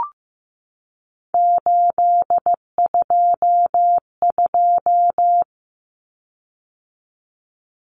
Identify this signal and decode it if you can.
Morse code, heard as a keyed tone, about 15 wpm, 700 Hz. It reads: 822
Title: Common Cut Numbers - 3-Digit 15wpm